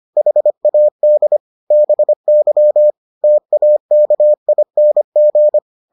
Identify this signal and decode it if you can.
Morse code, heard as a keyed tone, about 25 wpm, 600 Hz. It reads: HAD BY TAKING